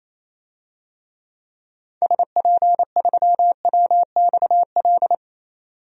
Morse code, heard as a keyed tone, about 28 wpm, 700 Hz. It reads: SP3WXL
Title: DXCC-100 Call Signs - 6 Characters - 28wpm